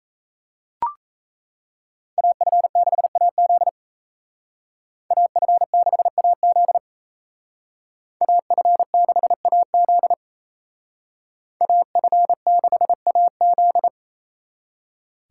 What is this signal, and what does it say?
Morse code, heard as a keyed tone, about 37 wpm, 700 Hz. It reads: AF6AZ AF6AZ AF6AZ AF6AZ